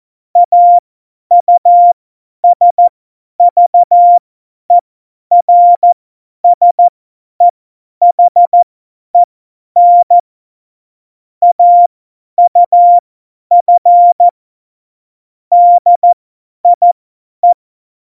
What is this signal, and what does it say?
Morse code, heard as a keyed tone, about 14 wpm, spaced as for 10 wpm, 700 Hz. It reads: AUSVERSEHEN AUF DIE
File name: Morsekurs / Witze / Witz 11. Dezember, 50 F70